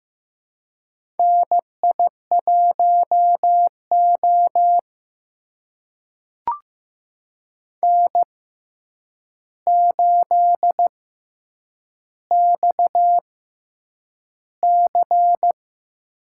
Morse code, heard as a keyed tone, about 15 wpm, spaced as for 8 wpm, 700 Hz. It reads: NI1O N8XC
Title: USA Call Signs - 4 Characters - 6x Character Spacing - 15wpm